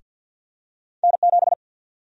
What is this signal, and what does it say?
Morse code, heard as a keyed tone, about 52 wpm, 700 Hz. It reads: N7
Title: USA Call Sign Prefixes - Repeated 2x - 50wpm